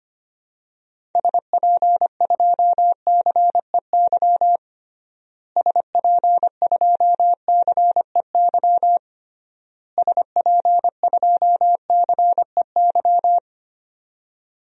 Morse code, heard as a keyed tone, about 25 wpm, 700 Hz. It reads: SP2CEY SP2CEY SP2CEY
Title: DXCC-100 Call Signs - 6 Characters - Repeated 3x - 25wpm